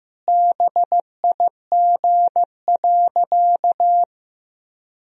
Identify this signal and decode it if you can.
Morse code, heard as a keyed tone, about 15 wpm, 700 Hz. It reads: BIG.
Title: Sentences from Top 1000 Words - Review Part 2 - 15wpm